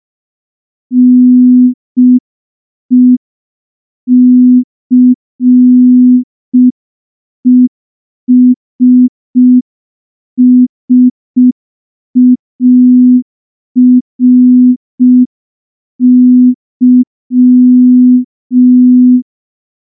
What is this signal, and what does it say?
Morse code, heard as a keyed tone, about 5 wpm, 250 Hz. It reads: NECESSARY
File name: morse / morse-training-05WPM